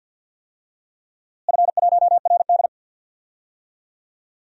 Morse code, heard as a keyed tone, about 51 wpm, 700 Hz. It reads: F1RD
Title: DXCC-100 Call Signs - 4 Characters - Repeated 2x - 50wpm